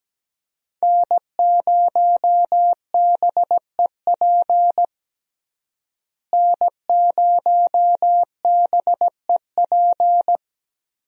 Morse code, heard as a keyed tone, about 17 wpm, 700 Hz. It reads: N0BEP N0BEP
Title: USA Call Signs - 5 Characters - Repeated 3x - 17wpm